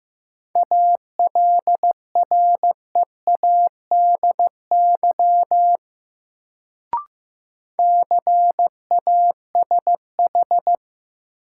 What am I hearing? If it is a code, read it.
Morse code, heard as a keyed tone, about 15 wpm, 700 Hz. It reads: ALREADY CASH
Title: Sentences from Top 700 Words - New Word Drill - 15wpm